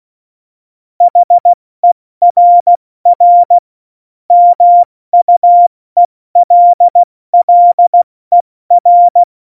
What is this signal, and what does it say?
Morse code, heard as a keyed tone, about 16 wpm, spaced as for 14 wpm, 700 Hz. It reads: HERR MUELLER
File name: Morsekurs / Witze / Witz 17. November 2020, 70 F80